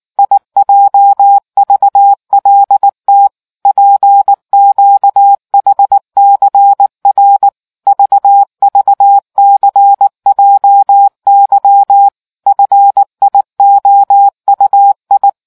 Morse code, heard as a keyed tone, about 19 wpm, 800 Hz. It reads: IJVLT PQHCR VVCJY FIOUI